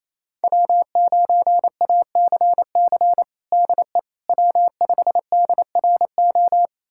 Morse code, heard as a keyed tone, about 28 wpm, 700 Hz. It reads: W9ACC DE W5DRO
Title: USA Callsign DE Callsign - 5 Characters - 28wpm